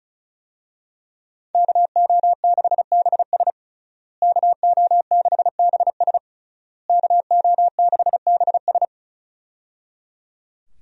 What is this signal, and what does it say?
Morse code, heard as a keyed tone, about 36 wpm, 700 Hz. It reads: KO6BS KO6BS KO6BS
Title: USA Call Signs - 5 Characters - Repeated 3x - 35wpm